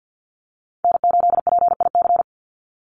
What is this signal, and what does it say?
Morse code, heard as a keyed tone, about 50 wpm, 700 Hz. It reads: N8PIC
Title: USA Call Signs 50wpm